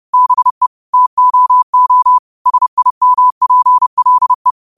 Morse code, heard as a keyed tone, about 30 wpm, 1000 Hz. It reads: BE TOO SIMPLE
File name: qotdfiles / 30wpm